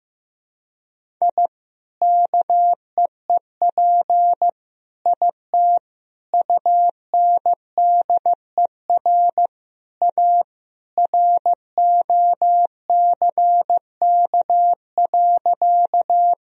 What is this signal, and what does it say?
Morse code, heard as a keyed tone, about 15 wpm, 700 Hz. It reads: I KEEP IT UNDER A ROCK.
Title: Sentences from Top 1000 Words - Review Part 2 - 15wpm